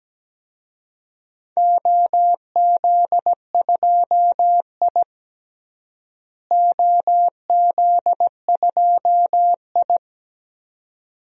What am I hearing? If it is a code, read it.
Morse code, heard as a keyed tone, about 17 wpm, 700 Hz. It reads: OZ2I OZ2I
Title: DXCC-100 Call Signs - 4 Characters - Repeated 2x - 17wpm